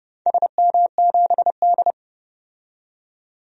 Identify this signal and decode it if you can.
Morse code, heard as a keyed tone, about 31 wpm, 700 Hz. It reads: SM7D
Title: DXCC-100 Call Signs - 4 Characters - 30wpm